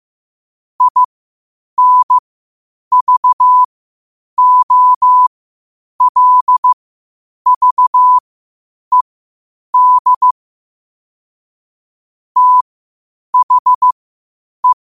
Morse code, heard as a keyed tone, about 15 wpm, spaced as for 8 wpm, 1000 Hz. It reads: INVOLVED THE
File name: qotdfiles / 8wpm